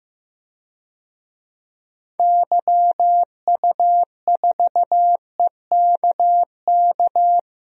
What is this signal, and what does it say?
Morse code, heard as a keyed tone, about 15 wpm, 700 Hz. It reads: YU4EKK
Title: DXCC-100 Call Signs - 6 Characters - 15wpm